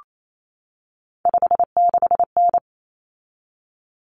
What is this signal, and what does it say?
Morse code, heard as a keyed tone, about 28 wpm, 700 Hz. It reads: 56N
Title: Common Cut Numbers - 3-Digit 28wpm